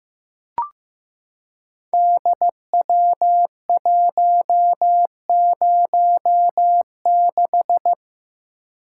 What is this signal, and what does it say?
Morse code, heard as a keyed tone, about 15 wpm, 700 Hz. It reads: DW106
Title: Code Groups - 5 Letter:Numbers - 15wpm